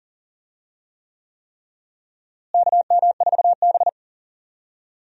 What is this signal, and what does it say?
Morse code, heard as a keyed tone, about 40 wpm, 700 Hz. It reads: KM4B
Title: USA Call Signs - 4 Characters - Repeated 3x - 40wpm